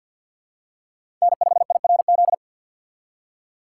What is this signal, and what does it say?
Morse code, heard as a keyed tone, about 53 wpm, 700 Hz. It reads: N5IRZ
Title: USA Call Signs - 5 Characters - 8x Character Spacing - 50wpm